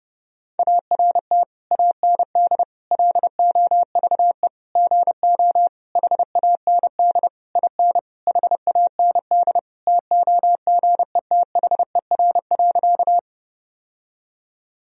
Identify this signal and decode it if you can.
Morse code, heard as a keyed tone, about 30 wpm, 700 Hz. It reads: ART AND LOVE GO HAND IN HAND TOGETHER.
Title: Sentences from Top 1000 Words - Review Part 3 - 30wpm